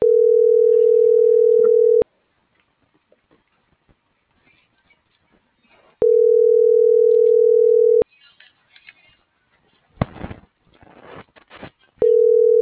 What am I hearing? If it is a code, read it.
Ringback tone (North America), 440 + 480 Hz.